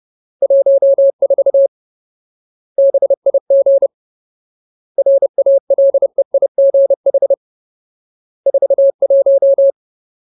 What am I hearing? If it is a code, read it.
Morse code, heard as a keyed tone, about 30 wpm, 550 Hz. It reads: 14 BIG RALEIGH 41
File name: Ultimate Word List / 30WPM Ultimate